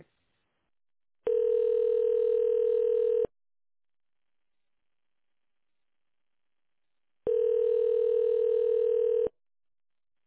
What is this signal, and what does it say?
Ringback tone (North America), 440 + 480 Hz.